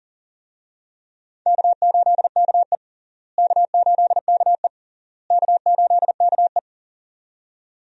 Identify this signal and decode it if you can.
Morse code, heard as a keyed tone, about 40 wpm, 700 Hz. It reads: K8KE K8KE K8KE
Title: USA Call Signs - 4 Characters - Repeated 3x - 40wpm